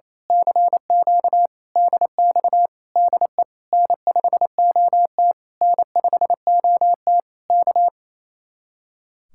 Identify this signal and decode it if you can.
Morse code, heard as a keyed tone, about 27 wpm, 700 Hz. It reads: CQ DX DE N5OT N5OT K
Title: Speed-Racing Common QSO Phrases - 28wpm